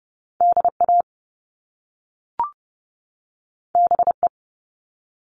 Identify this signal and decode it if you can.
Morse code, heard as a keyed tone, about 30 wpm, 700 Hz. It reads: DA BE
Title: Top Two Letter Combinations in Words 30wpm